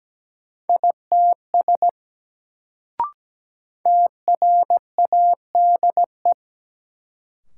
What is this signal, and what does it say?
Morse code, heard as a keyed tone, about 17 wpm, 700 Hz. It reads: ITS TRADE
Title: Sentences from Top 600 Words - New Word Drill - 17wpm